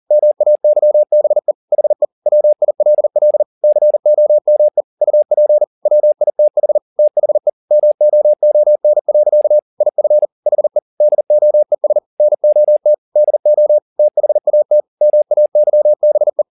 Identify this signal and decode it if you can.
Morse code, heard as a keyed tone, about 40 wpm, 600 Hz. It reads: MAYBE HE WILL COME UP WITH THE MOON. IF HE DOES NOT DO THAT MAYBE